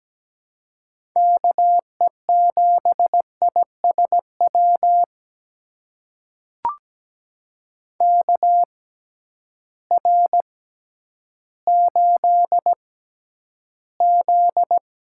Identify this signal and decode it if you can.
Morse code, heard as a keyed tone, about 17 wpm, spaced as for 10 wpm, 700 Hz. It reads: KE7ISW KR8Z
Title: USA Call Signs - 6 Characters - 6x Character Spacing - 17wpm